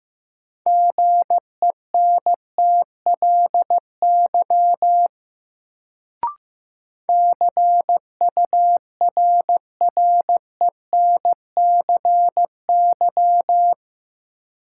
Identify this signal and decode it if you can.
Morse code, heard as a keyed tone, about 15 wpm, 700 Hz. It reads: GENTLY CURRENCY
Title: Sentences from Top 1500 Words - New Word Drill - 15wpm